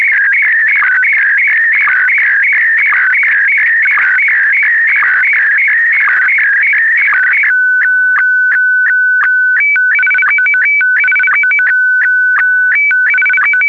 SSTV (slow-scan television), Scottie DX.
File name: scottie dx